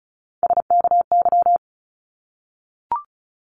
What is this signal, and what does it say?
Morse code, heard as a keyed tone, about 36 wpm, 700 Hz. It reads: SKY
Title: Multi-letter Phonemes 35wpm